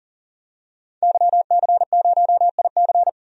Morse code, heard as a keyed tone, about 40 wpm, 700 Hz. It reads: YC0IC
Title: DXCC-100 Call Signs - 5 Characters - 40wpm